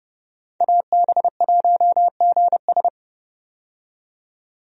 Morse code, heard as a keyed tone, about 30 wpm, 700 Hz. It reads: AB1GS
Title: USA Call Signs - 5 Characters - 30wpm